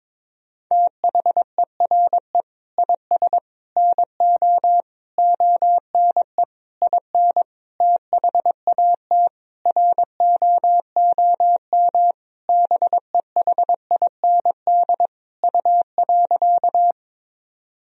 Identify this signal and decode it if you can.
Morse code, heard as a keyed tone, about 21 wpm, 700 Hz. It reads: THERE IS NO ONE IN THAT ROOM BEHIND U.
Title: Sentences from Top 500 Words and Abbreviations 22wpm